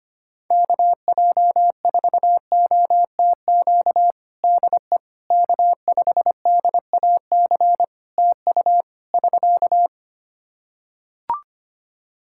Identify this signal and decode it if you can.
Morse code, heard as a keyed tone, about 25 wpm, 700 Hz. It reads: KJ4OTQ DE K5DAC TU <SK>
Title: Practice QSO - Exchange 4 - Ending - 25wpm